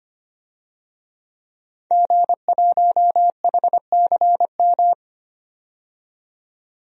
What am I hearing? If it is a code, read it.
Morse code, heard as a keyed tone, about 25 wpm, 700 Hz. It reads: G1HCM
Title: DXCC-100 Call Signs - 5 Characters - 25wpm